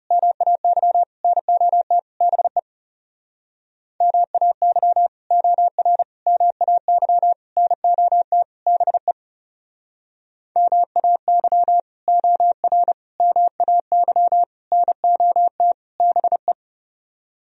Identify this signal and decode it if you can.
Morse code, heard as a keyed tone, about 37 wpm, 700 Hz. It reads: MAY NOT BE MAY OR MAY NOT BE MAY OR MAY NOT BE
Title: Speed-Racing - Sets of 5 Words 30wpm